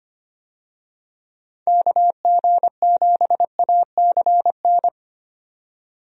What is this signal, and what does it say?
Morse code, heard as a keyed tone, about 25 wpm, 700 Hz. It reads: KG7ACN
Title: USA Call Signs - 6 Characters - 25wpm